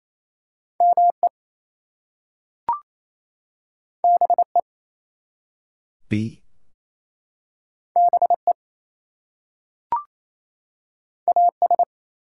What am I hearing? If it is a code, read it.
Morse code, heard as a keyed tone, about 28 wpm, 700 Hz. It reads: ME BE BE AS
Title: Two Letter Words - 28wpm